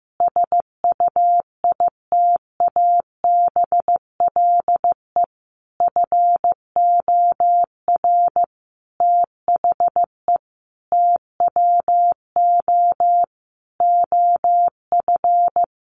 Morse code, heard as a keyed tone, about 15 wpm, 700 Hz. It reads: SUITABLE FOR THE TWO OF
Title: Sentences from Top 1400 Words 15wpm